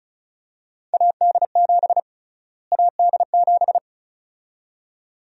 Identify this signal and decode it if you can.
Morse code, heard as a keyed tone, about 34 wpm, 700 Hz. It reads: AD7 AD7
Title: USA Call Sign Prefixes - Repeated 2x - 35wpm